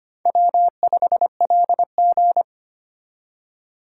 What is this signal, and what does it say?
Morse code, heard as a keyed tone, about 31 wpm, 700 Hz. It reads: W5LG
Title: Speed Racing - USA Call Signs 25wpm